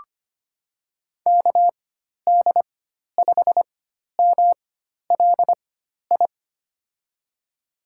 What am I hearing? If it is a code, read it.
Morse code, heard as a keyed tone, about 25 wpm, spaced as for 12 wpm, 700 Hz. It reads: KD5MLI
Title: USA Call Signs - 6 Characters - 4x Character Spacing - 25wpm